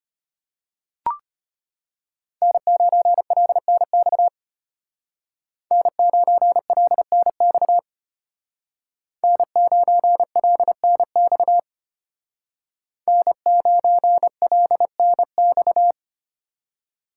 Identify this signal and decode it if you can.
Morse code, heard as a keyed tone, about 30 wpm, 700 Hz. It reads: N9LNX N9LNX N9LNX N9LNX
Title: Speed Racing - USA Call Signs 25wpm